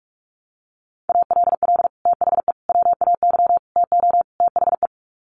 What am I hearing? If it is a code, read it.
Morse code, heard as a keyed tone, about 44 wpm, 700 Hz. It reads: ALL THE WAY TO THE
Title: Sets of 5 Words 45wpm